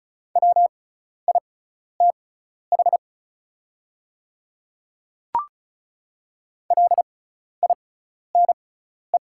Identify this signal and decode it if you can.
Morse code, heard as a keyed tone, about 34 wpm, spaced as for 12 wpm, 700 Hz. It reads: WITH LINE
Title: 6x Character Spacing - Top 500 Words - 35wpm